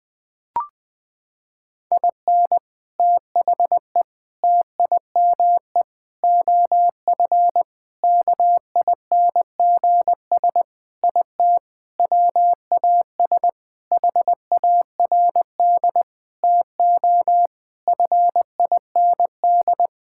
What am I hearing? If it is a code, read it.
Morse code, heard as a keyed tone, about 20 wpm, 700 Hz. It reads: IN THE TIME OF KINGS IT WAS HARD TO FIND
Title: Sentences with 3-Word Phrases - 20wpm